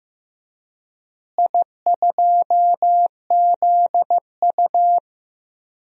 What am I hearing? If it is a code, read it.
Morse code, heard as a keyed tone, about 15 wpm, 700 Hz. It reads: I2ZU
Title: DXCC-100 Call Signs - 4 Characters - Repeated 3x - 15wpm